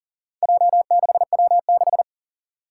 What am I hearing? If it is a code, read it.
Morse code, heard as a keyed tone, about 40 wpm, 690 Hz. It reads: J6W6
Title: Four Challenging Letter-Number Combinations 40wpm